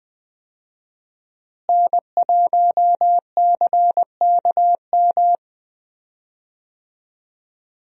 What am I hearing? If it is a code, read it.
Morse code, heard as a keyed tone, about 20 wpm, 700 Hz. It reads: N1CKM
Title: USA Call Signs - 5 Characters - Repeated 3x - 20wpm